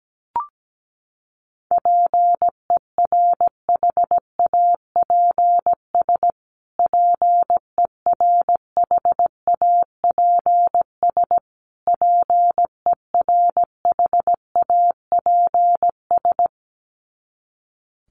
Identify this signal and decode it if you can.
Morse code, heard as a keyed tone, about 17 wpm, 700 Hz. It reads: PERHAPS PERHAPS PERHAPS
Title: Top 500 Words Repeated 17wpm